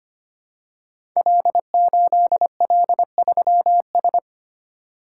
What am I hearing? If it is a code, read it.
Morse code, heard as a keyed tone, about 25 wpm, 700 Hz. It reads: L8L3S
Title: Code Groups - 5 Letter:Numbers - 25wpm